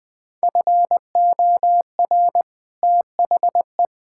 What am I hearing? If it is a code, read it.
Morse code, heard as a keyed tone, about 20 wpm, 700 Hz. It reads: FOR THE
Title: Build A Phrase - 2-Word Phrases - 20wpm